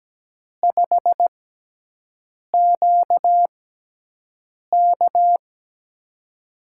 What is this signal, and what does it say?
Morse code, heard as a keyed tone, about 17 wpm, spaced as for 6 wpm, 700 Hz. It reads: 5QK
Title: USA Call Signs - 4 Characters - 6x Character Spacing - 17wpm